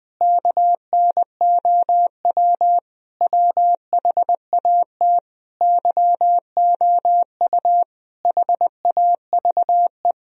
Morse code, heard as a keyed tone, about 20 wpm, 700 Hz. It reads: KNOW WHAT YOU HAVE
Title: Sentences from Top 1000 Words - Review Part 3 - 20wpm